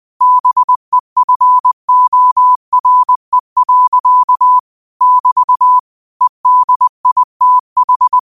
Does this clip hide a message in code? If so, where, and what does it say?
Yes: Morse code from 0.2 to 8.2 s, heard as a keyed tone, about 20 wpm, 1000 Hz. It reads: BEFORE. <BT> EDITH